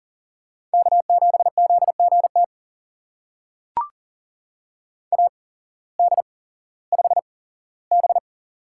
Morse code, heard as a keyed tone, about 40 wpm, spaced as for 19 wpm, 700 Hz. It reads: K7ZGT AD5B
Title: USA Call Signs - 8x Character Spacing - 40wpm